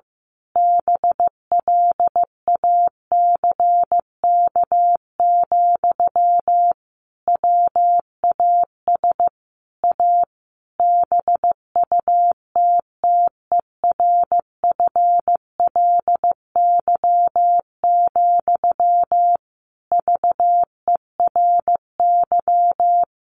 Morse code, heard as a keyed tone, about 14 wpm, 700 Hz. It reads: BLACK, WAS A BUTTERFLY, VERY
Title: A Sound Of Thunder 15wpm